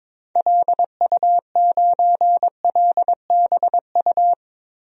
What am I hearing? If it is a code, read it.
Morse code, heard as a keyed tone, about 22 wpm, 700 Hz. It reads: LU9LBU
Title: DXCC-100 Call Signs - 6 Characters - Repeated 2x - 22wpm